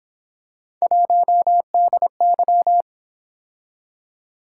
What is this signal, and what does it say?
Morse code, heard as a keyed tone, about 27 wpm, 700 Hz. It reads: 1DY
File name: Speed-Racing - Three Letter-Number Combinations - 22wpm